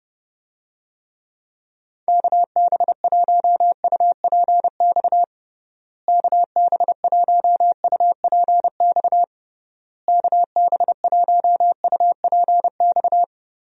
Morse code, heard as a keyed tone, about 32 wpm, 700 Hz. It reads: KB1UPX KB1UPX KB1UPX
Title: USA Call Signs - 6 Characters - Repeated 3x - 30wpm